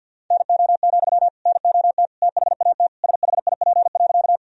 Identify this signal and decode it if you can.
Morse code, heard as a keyed tone, about 50 wpm, 700 Hz. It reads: NO, NOT THAT SHIP.
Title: Sentences from Top 1000 Words - Review Part 2 - 50wpm